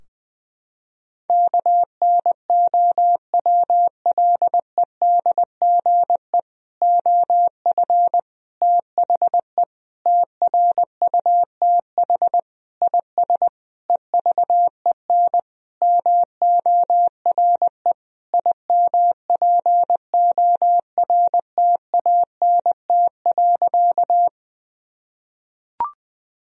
Morse code, heard as a keyed tone, about 20 wpm, 700 Hz. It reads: KNOWLEDGE OF THE TRUTH IS EVEN MORE IMPORTANT.
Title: Sentences from Top 1000 Words - Review Part 3 - 20wpm